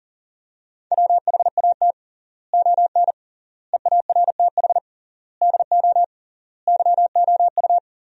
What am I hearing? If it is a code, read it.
Morse code, heard as a keyed tone, about 40 wpm, 700 Hz. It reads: WHAT ON EARTH DO YOU
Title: 3x Word-Spacing - Sentences from Top 200 Words - 40wpm